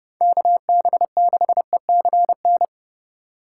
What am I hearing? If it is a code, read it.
Morse code, heard as a keyed tone, about 31 wpm, 700 Hz. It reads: KB6ECN
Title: USA Call Signs - 6 Characters - 4x Character Spacing - 30wpm